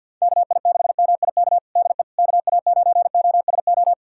Morse code, heard as a keyed tone, about 53 wpm, 700 Hz. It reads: KI6MIK DE KA9OSO
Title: USA Callsign DE Callsign - 6 Characters - 50wpm